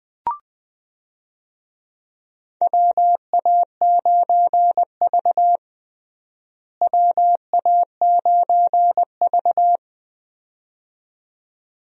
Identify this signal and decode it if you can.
Morse code, heard as a keyed tone, about 20 wpm, 700 Hz. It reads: WA9V WA9V
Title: USA Call Signs - 4 Characters - Repeated 2x - 20wpm